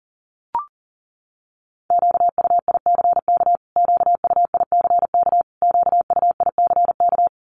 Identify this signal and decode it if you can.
Morse code, heard as a keyed tone, about 40 wpm, 700 Hz. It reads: QUICK QUICK QUICK
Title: Top 500 Words Repeated 40wpm